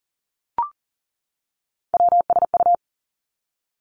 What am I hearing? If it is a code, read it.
Morse code, heard as a keyed tone, about 38 wpm, 700 Hz. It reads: WSU
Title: Three Challenging Letter-Number Combinations 40wpm